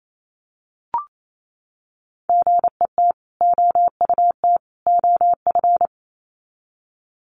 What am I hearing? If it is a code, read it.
Morse code, heard as a keyed tone, about 28 wpm, 700 Hz. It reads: GET OUT OF
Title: Sets of 3 Words 28wpm